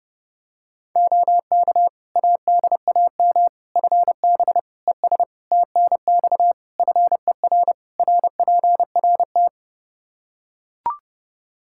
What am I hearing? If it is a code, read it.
Morse code, heard as a keyed tone, about 32 wpm, 700 Hz. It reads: OK ADAM FB ES TNX FER RPRT
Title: Practice QSO - Exchange 2 - Info - 30wpm